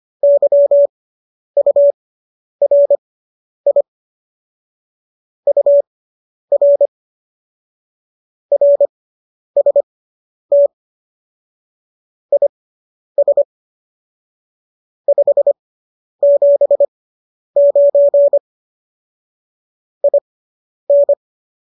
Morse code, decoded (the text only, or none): YURI UR RST IS 579 IN